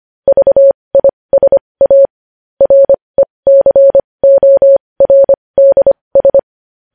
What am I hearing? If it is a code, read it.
Morse code, heard as a keyed tone, about 25 wpm, spaced as for 20 wpm, 550 Hz. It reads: VISA RECORDS
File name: podcast / News headlines-1x-25x20